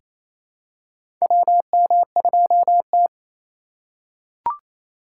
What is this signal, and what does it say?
Morse code, heard as a keyed tone, about 28 wpm, 700 Hz. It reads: WM2T
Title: USA Call Signs - 4x Character Spacing - 28wpm